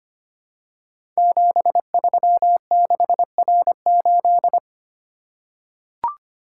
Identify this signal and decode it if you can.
Morse code, heard as a keyed tone, about 25 wpm, 700 Hz. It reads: 736R8
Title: Code Groups - 5 Letter:Numbers - 25wpm